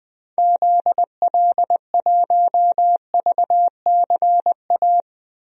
Morse code, heard as a keyed tone, about 21 wpm, 700 Hz. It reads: ZL1VCA
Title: DXCC-100 Call Signs - 6 Characters - 20wpm